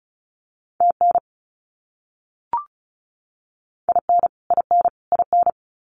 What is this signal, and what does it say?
Morse code, heard as a keyed tone, about 34 wpm, 700 Hz. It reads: TN IN IN IN
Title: US States Repeated 3x 35wpm